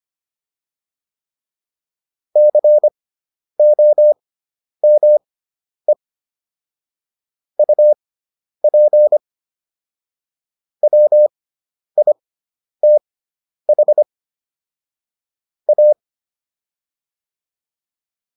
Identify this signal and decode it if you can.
Morse code, heard as a keyed tone, about 25 wpm, spaced as for 10 wpm, 600 Hz. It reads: COME UP WITH A